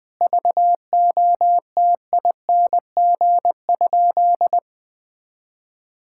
Morse code, heard as a keyed tone, about 20 wpm, 700 Hz. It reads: VOTING?
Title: Sentences with 9-Word Phrases - 20wpm